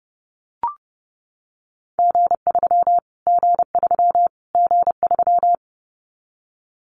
Morse code, heard as a keyed tone, about 30 wpm, 700 Hz. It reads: G3 G3 G3
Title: Callsign Prefixes for Top 25 DXCC - Repeated 3x - 30wpm